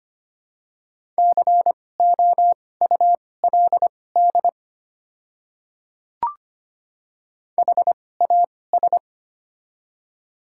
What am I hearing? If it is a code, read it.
Morse code, heard as a keyed tone, about 25 wpm, spaced as for 18 wpm, 700 Hz. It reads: COULD HAS
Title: 2x Character Spacing - Top 100 Words - 25wpm